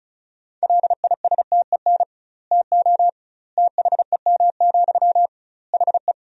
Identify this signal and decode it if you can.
Morse code, heard as a keyed tone, about 36 wpm, 700 Hz. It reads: LISTEN TO THEM, HE
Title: 2x Word-Spacing - Sentences from Top 400 Words - 35wpm